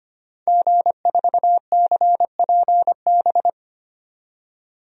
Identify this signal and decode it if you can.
Morse code, heard as a keyed tone, about 25 wpm, 700 Hz. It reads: G4CPB
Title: DXCC-100 Call Signs - 5 Characters - 25wpm